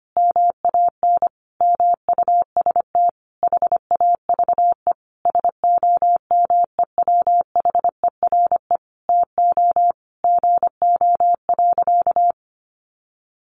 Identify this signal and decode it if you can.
Morse code, heard as a keyed tone, about 25 wpm, 700 Hz. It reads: MAN MUST HAVE SOMEWHERE TO GO.
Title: Sentences from Top 1400 Words 25wpm